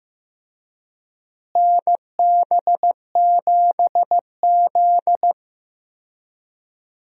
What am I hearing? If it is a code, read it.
Morse code, heard as a keyed tone, about 15 wpm, 700 Hz. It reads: NB7Z
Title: USA Call Signs - 4 Characters - Repeated 2x - 15wpm